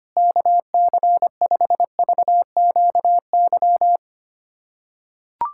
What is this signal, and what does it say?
Morse code, heard as a keyed tone, about 25 wpm, 700 Hz. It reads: KC5VQY
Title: USA Callsign DE Callsign - 6 Characters - 25wpm